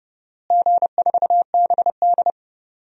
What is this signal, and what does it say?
Morse code, heard as a keyed tone, about 30 wpm, 700 Hz. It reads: G4BD
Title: DXCC-100 Call Signs - 4 Characters - Repeated 3x - 30wpm